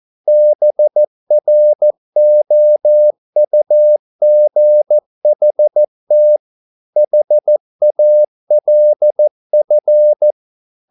Morse code, heard as a keyed tone, about 14 wpm, 600 Hz. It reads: BROUGHT HALF
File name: mp3 files / a dogs mistake - 14wpm